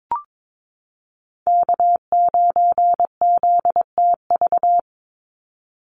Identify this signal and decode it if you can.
Morse code, heard as a keyed tone, about 22 wpm, 700 Hz. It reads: K9ZTV
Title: Call Signs Repeated 2x - 22wpm